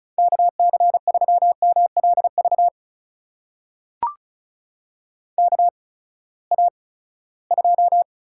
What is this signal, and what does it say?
Morse code, heard as a keyed tone, about 36 wpm, spaced as for 20 wpm, 700 Hz. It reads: KC3MLV KA2
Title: USA Call Signs - 6 Characters - 8x Character Spacing - 35wpm